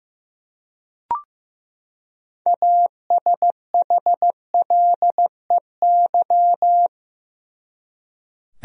Morse code, heard as a keyed tone, about 15 wpm, 700 Hz. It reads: ASHLEY
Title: USA Common Names - 15wpm